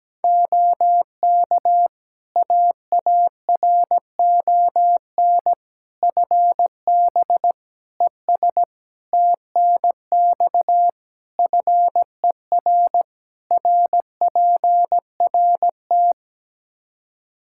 Morse code, heard as a keyed tone, about 17 wpm, 700 Hz. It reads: OK AARON FB ES TNX FER RPRT
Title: K1USN SST QSO Practice 17wpm